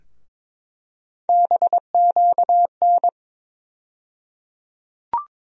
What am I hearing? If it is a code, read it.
Morse code, heard as a keyed tone, about 22 wpm, 700 Hz. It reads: BQN
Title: USA Call Sign Postfixes - 3 Letters 22wpm